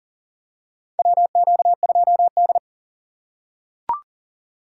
Morse code, heard as a keyed tone, about 40 wpm, 700 Hz. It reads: WQ2D
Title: USA Call Signs - 2x Character Spacing - 40wpm